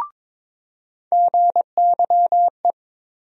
Morse code, heard as a keyed tone, about 21 wpm, 700 Hz. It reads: GYE
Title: USA Call Sign Postfixes - 3 Letters 22wpm